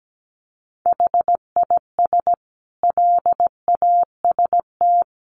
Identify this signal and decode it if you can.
Morse code, heard as a keyed tone, about 17 wpm, 700 Hz. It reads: HIS LAST
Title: Sentences from Top 1200 Words 17wpm